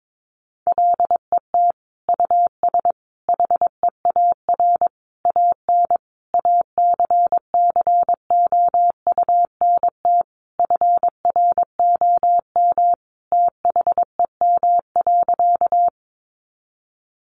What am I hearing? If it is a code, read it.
Morse code, heard as a keyed tone, about 22 wpm, 700 Hz. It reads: LET US HEAR AN ACCOUNT FROM THEM.
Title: Sentences from Top 900 Words 22wpm